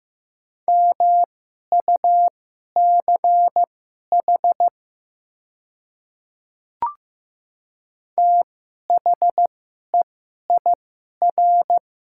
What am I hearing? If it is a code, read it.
Morse code, heard as a keyed tone, about 15 wpm, spaced as for 11 wpm, 700 Hz. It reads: MUCH THEIR
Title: 2x Character Spacing - Top 200 Words - 15wpm